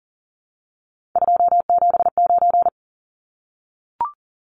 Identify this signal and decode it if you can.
Morse code, heard as a keyed tone, about 40 wpm, 700 Hz. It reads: 279
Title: Numbers - 3-Digit 40wpm